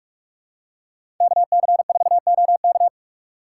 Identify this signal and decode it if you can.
Morse code, heard as a keyed tone, about 46 wpm, 700 Hz. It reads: KC4OK
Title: USA Call Signs - 6x Character Spacing - 45wpm